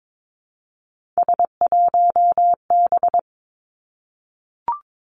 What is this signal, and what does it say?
Morse code, heard as a keyed tone, about 22 wpm, 700 Hz. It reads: S1B
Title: Three Challenging Letter-Number Combinations 22wpm